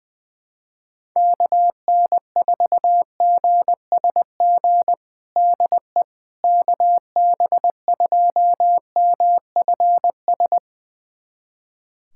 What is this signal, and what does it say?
Morse code, heard as a keyed tone, about 20 wpm, 700 Hz. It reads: KN4GSG DE KB2MFS
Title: USA Callsign DE Callsign - 6 Characters - 20wpm